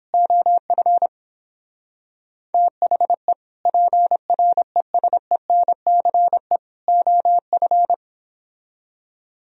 Morse code, heard as a keyed tone, about 29 wpm, 700 Hz. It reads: OF THE PRESENCE OF